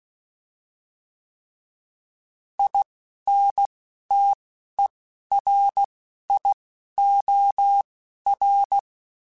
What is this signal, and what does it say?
Morse code, heard as a keyed tone, about 16 wpm, spaced as for 11 wpm, 800 Hz. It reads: INTERIOR
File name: Alphabetical / Alphabetical mp3s / Is @10